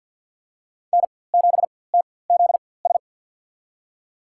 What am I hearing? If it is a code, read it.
Morse code, heard as a keyed tone, about 52 wpm, spaced as for 23 wpm, 700 Hz. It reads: N7TZS
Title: USA Call Signs - 5 Characters - 4x Character Spacing - 50wpm